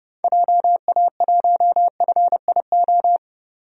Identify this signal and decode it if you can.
Morse code, heard as a keyed tone, about 30 wpm, 700 Hz. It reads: JA1FIO
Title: DXCC-100 Call Signs - 6 Characters - 30wpm